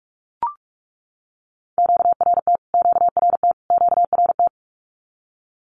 Morse code, heard as a keyed tone, about 44 wpm, 700 Hz. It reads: QRT QRT QRT
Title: Q-Codes Repeated 3x 45wpm